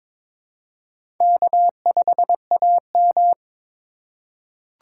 Morse code, heard as a keyed tone, about 21 wpm, 700 Hz. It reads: K5AM
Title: K1USN SST QSO Practice - 22wpm